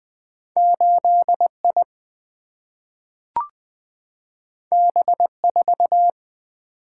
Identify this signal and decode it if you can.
Morse code, heard as a keyed tone, about 20 wpm, 700 Hz. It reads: 8I B4
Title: Single-Letter Single-Number 20wpm